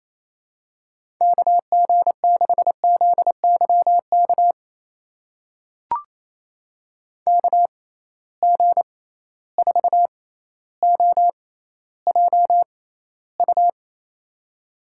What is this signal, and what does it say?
Morse code, heard as a keyed tone, about 28 wpm, spaced as for 14 wpm, 700 Hz. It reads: KG6ZYK KG4OJU
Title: USA Call Signs - 6 Characters - 6x Character Spacing - 28wpm